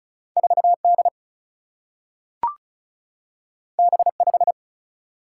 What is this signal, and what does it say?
Morse code, heard as a keyed tone, about 36 wpm, 700 Hz. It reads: 4 D B 5
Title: Single-Letter Single-Number 35wpm